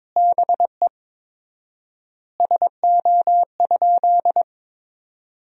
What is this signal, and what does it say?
Morse code, heard as a keyed tone, about 22 wpm, 700 Hz. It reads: BE SO?
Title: 4x Word-Spacing - Sentences from Top 300 Words - 22wpm